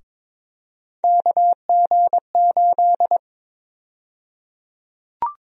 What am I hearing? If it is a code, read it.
Morse code, heard as a keyed tone, about 22 wpm, 700 Hz. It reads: KG8
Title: USA Call Sign Prefixes - 22wpm